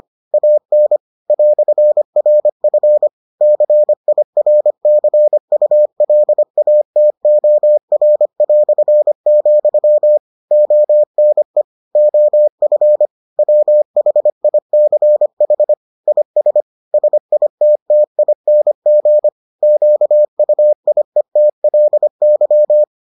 Morse code, heard as a keyed tone, about 25 wpm, 600 Hz. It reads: AN "RF CIRCULATOR", ONE OF WHICH IS SITTING QUIETLY